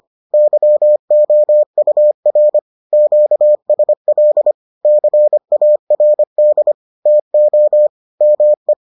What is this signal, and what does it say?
Morse code, heard as a keyed tone, about 25 wpm, 600 Hz. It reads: YOUR QSL CARD TO ME